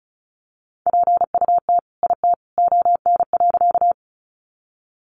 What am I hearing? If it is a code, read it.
Morse code, heard as a keyed tone, about 34 wpm, 700 Hz. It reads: PUT IT ON.
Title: Sentences from Top 200 Words 35wpm